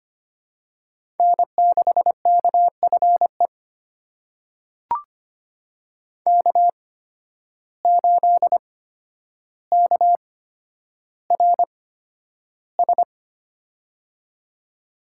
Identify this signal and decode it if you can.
Morse code, heard as a keyed tone, about 25 wpm, spaced as for 11 wpm, 700 Hz. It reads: N6KFE K8KRS
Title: USA Call Signs - 5 Characters - 8x Character Spacing - 25wpm